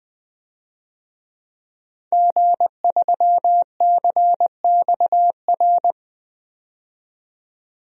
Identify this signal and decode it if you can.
Morse code, heard as a keyed tone, about 20 wpm, 700 Hz. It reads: G3CXR
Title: DXCC-100 Call Signs - 5 Characters - 20wpm